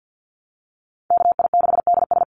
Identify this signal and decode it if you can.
Morse code, heard as a keyed tone, about 50 wpm, 700 Hz. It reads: KI6DS
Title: NA SOTA Chaser Callsigns 50wpm